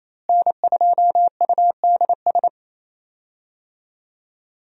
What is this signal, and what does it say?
Morse code, heard as a keyed tone, about 28 wpm, 700 Hz. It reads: N2UDS
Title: USA Call Signs - 5 Characters - Repeated 2x - 28wpm